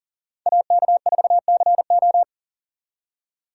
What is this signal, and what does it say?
Morse code, heard as a keyed tone, about 50 wpm, 700 Hz. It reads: AK4CO